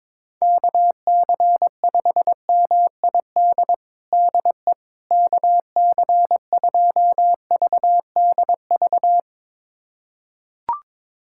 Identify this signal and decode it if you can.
Morse code, heard as a keyed tone, about 22 wpm, 700 Hz. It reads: KC5MID DE KC2VDV
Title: USA Callsign DE Callsign - 6 Characters - 22wpm